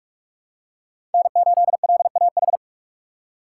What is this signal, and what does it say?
Morse code, heard as a keyed tone, about 46 wpm, 700 Hz. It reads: N8LAH
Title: USA Call Signs - 5 Characters - 3x Character Spacing - 45wpm